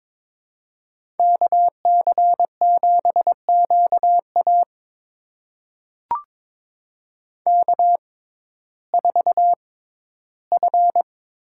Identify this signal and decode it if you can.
Morse code, heard as a keyed tone, about 22 wpm, spaced as for 14 wpm, 700 Hz. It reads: KC7QA K4F
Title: USA Call Signs - 5 Characters - 6x Character Spacing - 22wpm